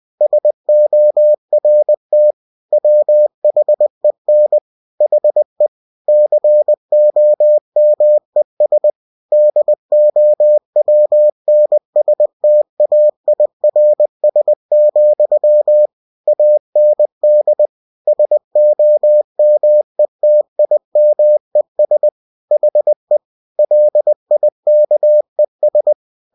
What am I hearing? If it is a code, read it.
Morse code, heard as a keyed tone, about 20 wpm, 600 Hz. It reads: SORT WHEN HE COMES DOWNSTAIRS, AND SOMETIMES HE LIKES